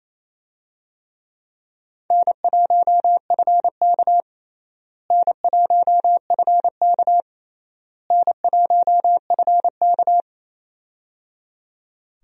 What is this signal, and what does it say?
Morse code, heard as a keyed tone, about 28 wpm, 700 Hz. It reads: N1FK N1FK N1FK
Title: USA Call Signs - 4 Characters - Repeated 3x - 28wpm